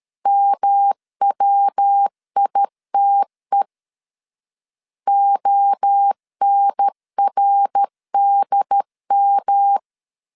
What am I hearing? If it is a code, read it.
Morse code, heard as a keyed tone, about 13 wpm, 800 Hz. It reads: MWITE ONRDM
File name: CW / DET / 1 3 1500C S C 12wpm 2750